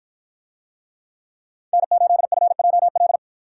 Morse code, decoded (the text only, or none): N8FJL